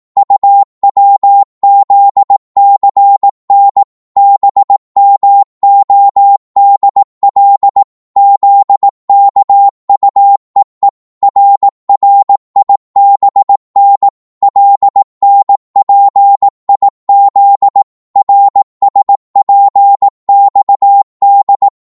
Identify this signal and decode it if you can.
Morse code, heard as a keyed tone, about 18 wpm, 800 Hz. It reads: UWZCN BMODL ZKUEE RRIBN LNPIZ RSPXD
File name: Morse / ltr-18wpm